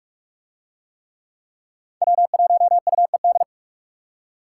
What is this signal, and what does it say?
Morse code, heard as a keyed tone, about 40 wpm, 700 Hz. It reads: W1UED